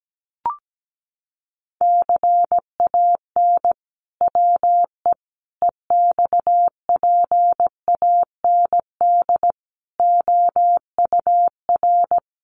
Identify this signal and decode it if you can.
Morse code, heard as a keyed tone, about 17 wpm, 700 Hz. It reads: CAN WE EXPAND OUR
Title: Sentences from Top 1400 Words 17wpm